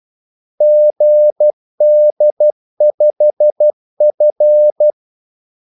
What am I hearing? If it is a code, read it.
Morse code, heard as a keyed tone, about 12 wpm, 600 Hz. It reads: GD5F
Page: 2 ekstra ordmellemrum mellem kaldesignalerne